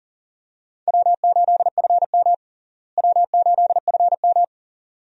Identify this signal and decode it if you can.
Morse code, heard as a keyed tone, about 40 wpm, 700 Hz. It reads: W8FM W8FM
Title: USA Call Signs - 4 Characters - Repeated 2x - 40wpm